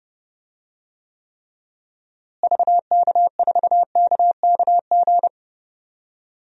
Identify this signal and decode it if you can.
Morse code, heard as a keyed tone, about 31 wpm, 700 Hz. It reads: VK4KKG
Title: DXCC-100 Call Signs - 6 Characters - Repeated 2x - 30wpm